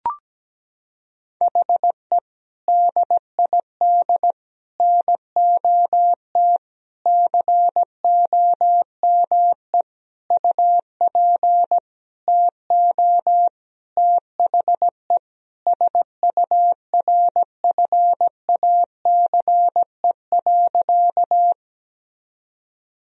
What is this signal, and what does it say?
Morse code, heard as a keyed tone, about 17 wpm, 700 Hz. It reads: HE DID NOT COME UP TO THE SURFACE.
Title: Sentences from Top 1000 Words - Review Part 3 - 17wpm